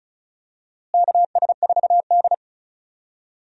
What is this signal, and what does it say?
Morse code, heard as a keyed tone, about 34 wpm, 700 Hz. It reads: KS4D
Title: USA Call Signs - 4 Characters - Repeated 3x - 35wpm